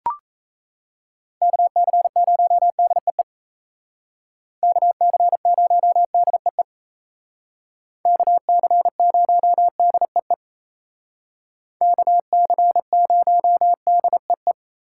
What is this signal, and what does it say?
Morse code, heard as a keyed tone, about 36 wpm, 700 Hz. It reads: KC0DEE KC0DEE KC0DEE KC0DEE